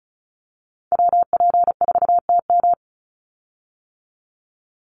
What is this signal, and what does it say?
Morse code, heard as a keyed tone, about 34 wpm, 700 Hz. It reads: WP4TM
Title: USA Call Signs 35wpm